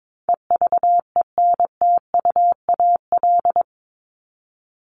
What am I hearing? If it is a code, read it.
Morse code, heard as a keyed tone, about 22 wpm, 700 Hz. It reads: EVENTUAL
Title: LY Suffix - Top 100 Words with Base Words 22wpm